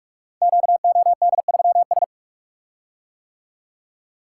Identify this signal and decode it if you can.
Morse code, heard as a keyed tone, about 44 wpm, 700 Hz. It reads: QOD3S
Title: Code Groups - 5 Letter:Numbers - 45wpm